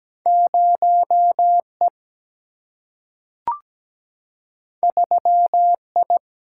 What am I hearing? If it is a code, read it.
Morse code, heard as a keyed tone, about 17 wpm, 700 Hz. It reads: 0E 3I
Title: Single-Letter Single-Number 17wpm